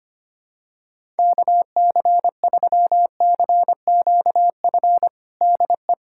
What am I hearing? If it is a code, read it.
Morse code, heard as a keyed tone, about 25 wpm, 700 Hz. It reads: KC3CQF DE
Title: USA Callsign DE Callsign - 6 Characters - 25wpm